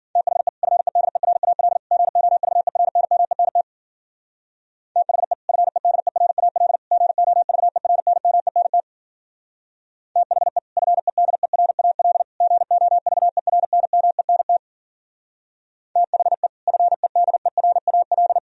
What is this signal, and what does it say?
Morse code, heard as a keyed tone, about 49 wpm, 700 Hz. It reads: THE FEDERAL GOVERNMENT THE FEDERAL GOVERNMENT THE FEDERAL GOVERNMENT THE FEDERAL
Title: Speed-Racing - Sets of 3 Words 40wpm